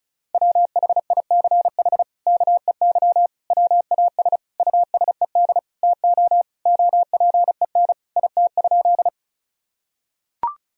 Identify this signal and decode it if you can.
Morse code, heard as a keyed tone, about 34 wpm, 700 Hz. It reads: WHICH KEY WAS USED TO OPEN IT?
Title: Sentences from Top 1000 Words - Review Part 4 - 35wpm